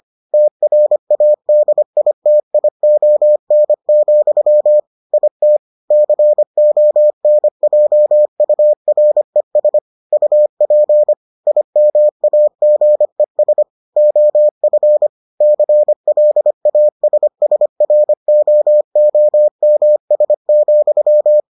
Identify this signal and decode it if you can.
Morse code, heard as a keyed tone, about 25 wpm, 600 Hz. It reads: TRADITION, IT CONJURES UP IMAGES OF CLASSROOMS,